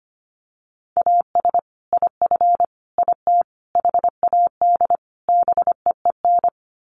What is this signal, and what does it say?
Morse code, heard as a keyed tone, about 25 wpm, 700 Hz. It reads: AS IF IT HAD BEEN
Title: Sets of 5 Words Encore2 25wpm